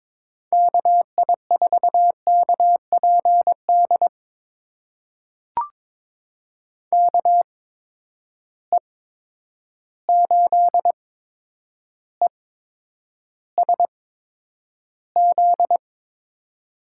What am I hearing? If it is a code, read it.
Morse code, heard as a keyed tone, about 22 wpm, spaced as for 9 wpm, 700 Hz. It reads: KI4KPD KE8ESZ
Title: USA Call Signs - 6 Characters - 8x Character Spacing - 22wpm